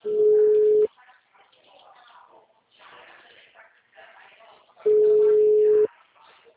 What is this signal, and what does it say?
Ringback tone (Europe, 425 Hz), with speech.
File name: out- -2001- - - .15281